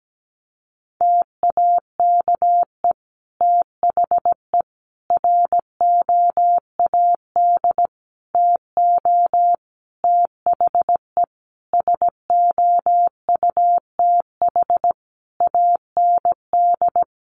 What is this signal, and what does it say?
Morse code, heard as a keyed tone, about 17 wpm, 700 Hz. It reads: TAKE THE ROAD TO THE SOUTH AND
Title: Sentences from Top 400 Words 17wpm